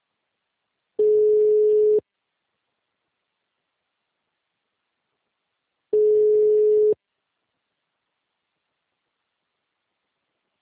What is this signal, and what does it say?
Ringback tone (Europe, 425 Hz).
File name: out- -2001- - - .6